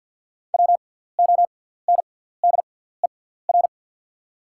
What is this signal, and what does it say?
Morse code, heard as a keyed tone, about 53 wpm, spaced as for 17 wpm, 690 Hz. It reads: WONDER
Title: 6x Character Spacing - Top 500 Words - 50wpm